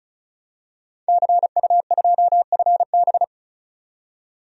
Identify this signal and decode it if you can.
Morse code, heard as a keyed tone, about 34 wpm, 700 Hz. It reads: CU2FB
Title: DXCC-100 Call Signs - 5 Characters - 35wpm